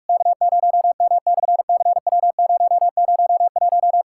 Morse code, heard as a keyed tone, about 44 wpm, 660 Hz. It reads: K0M/CW001
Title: SOTA QSO Practice 45wpm